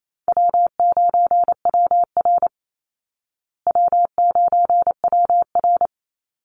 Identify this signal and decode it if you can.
Morse code, heard as a keyed tone, about 28 wpm, 700 Hz. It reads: W9WR W9WR
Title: Easy Call Signs Repeated 3x - 28wpm